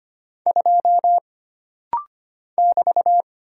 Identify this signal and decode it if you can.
Morse code, heard as a keyed tone, about 25 wpm, 700 Hz. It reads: 2 <BT>
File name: Single Character (Letter:Number:Punctuation:Prosign) - Rapid-Fire - 25wpm